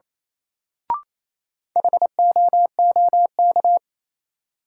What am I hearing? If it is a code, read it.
Morse code, heard as a keyed tone, about 28 wpm, 700 Hz. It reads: HOOK
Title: Sentences from Top 1600 Words - New Word Drill Alt - 28wpm